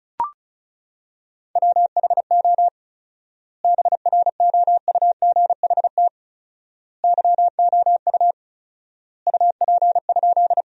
Morse code, heard as a keyed tone, about 34 wpm, 700 Hz. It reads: WHO BROUGHT YOU UP?
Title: 4x Word-Spacing - Sentences from Top 500 Words - 35wpm